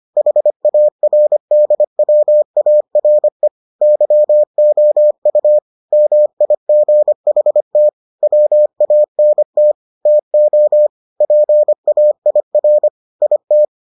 Morse code, heard as a keyed tone, about 25 wpm, 600 Hz. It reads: HARDWARE YOU MIGHT WANT TO PAIR IT